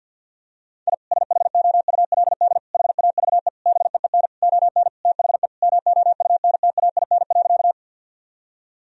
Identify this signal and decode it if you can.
Morse code, heard as a keyed tone, about 52 wpm, 700 Hz. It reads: I SHOULD HAVE BEEN ON THE MOUNTAIN.
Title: Sentences from Top 1000 Words - Review Part 1 - 50wpm